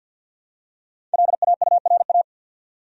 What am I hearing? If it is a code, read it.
Morse code, heard as a keyed tone, about 47 wpm, 700 Hz. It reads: LAURA
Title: USA Common Names - 50wpm